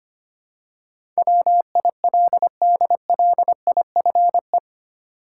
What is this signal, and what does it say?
Morse code, heard as a keyed tone, about 25 wpm, 700 Hz. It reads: WILDLIFE
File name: 8 Letter Words - First 3 Letters Given - 25wpm